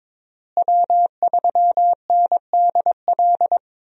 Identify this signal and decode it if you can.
Morse code, heard as a keyed tone, about 22 wpm, 700 Hz. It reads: W3NDL
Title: USA Call Signs - 5 Characters - Repeated 3x - 22wpm